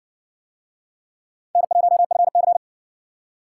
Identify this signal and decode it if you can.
Morse code, heard as a keyed tone, about 60 wpm, 700 Hz. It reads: N1FC